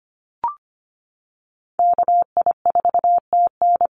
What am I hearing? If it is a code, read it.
Morse code, heard as a keyed tone, about 25 wpm, 700 Hz. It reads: KI4TN
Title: NA SOTA Activator Callsigns 25wpm